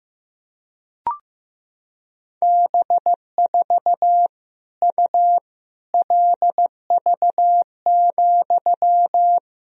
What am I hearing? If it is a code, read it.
Morse code, heard as a keyed tone, about 15 wpm, 700 Hz. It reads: B4 U LV,
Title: Sentences from Top 400 Words and Abbreviations 15wpm